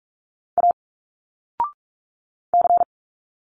Morse code, heard as a keyed tone, about 46 wpm, 700 Hz. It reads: A C
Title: Single Letters 45wpm